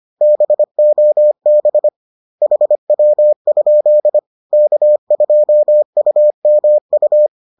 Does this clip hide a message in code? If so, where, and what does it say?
Yes: Morse code from 0.2 to 7.3 s, heard as a keyed tone, about 25 wpm, 600 Hz. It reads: BOB HW? K2UMU